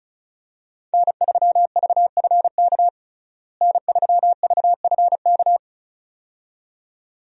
Morse code, heard as a keyed tone, about 34 wpm, 700 Hz. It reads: N3VFK N3VFK
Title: USA Call Signs - 5 Characters - Repeated 2x - 35wpm